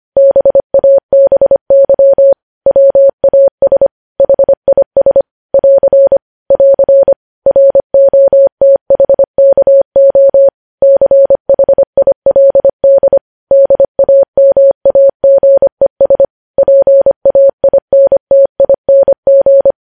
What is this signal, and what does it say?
Morse code, heard as a keyed tone, about 25 wpm, 550 Hz. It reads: BABY WAS HIS <AR> <AR> ROTHKO CHILD DAMAGES PAINTING
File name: podcast / News headlines-1x-25x25